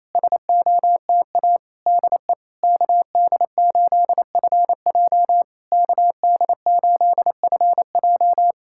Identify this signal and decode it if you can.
Morse code, heard as a keyed tone, about 28 wpm, 700 Hz. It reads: SOTA DE KD8FJ KD8FJ
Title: SOTA QSO Practice 28wpm